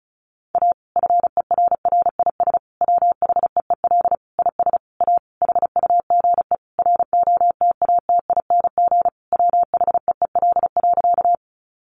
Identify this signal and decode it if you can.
Morse code, heard as a keyed tone, about 34 wpm, 700 Hz. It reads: A FERRIS WHEEL IS A HUGE ROTATING WHEEL.
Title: Additional Easy Phrases 35wpm